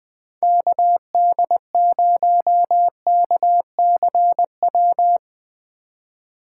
Morse code, heard as a keyed tone, about 20 wpm, 700 Hz. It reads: KD0KCW
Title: Speed Racing - USA 2x3 Call Signs 17wpm